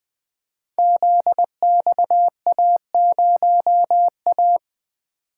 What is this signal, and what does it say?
Morse code, heard as a keyed tone, about 20 wpm, 700 Hz. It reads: ZXA0A
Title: Code Groups - 5 Letter:Numbers - 20wpm